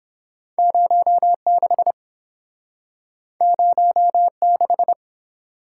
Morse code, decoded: 06 06

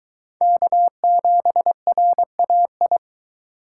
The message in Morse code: K7RAI

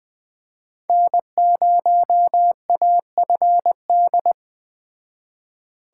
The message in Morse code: N0AFD

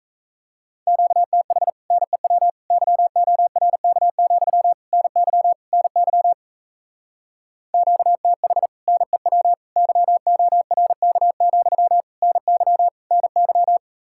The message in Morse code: QTH NEW YORK, NY NY QTH NEW YORK, NY NY

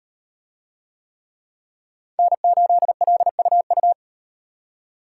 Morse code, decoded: N8LUU

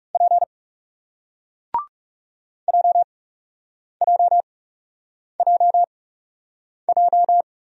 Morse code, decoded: P JJJJ

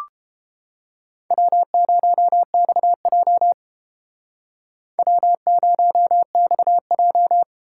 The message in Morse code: W0XJ W0XJ

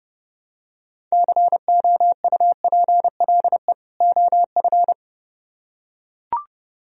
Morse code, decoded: COUPLE OF

